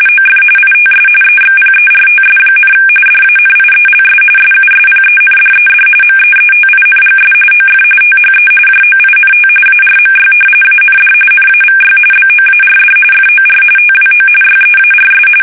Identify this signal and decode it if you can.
RTTY, about 45.45 baud. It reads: OF ITS EXPERTS, DANIELS SUSPENDED MEETINGS OF THE JOINT ARMY AND NAVY BOARD FOR TWO YEARS
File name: fsk850